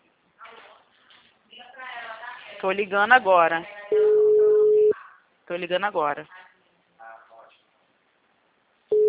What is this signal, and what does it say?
Ringback tone (Europe, 425 Hz), with speech.
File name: out- -2033- - - .51083